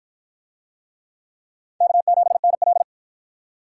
Morse code, heard as a keyed tone, about 52 wpm, 700 Hz. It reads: K7NL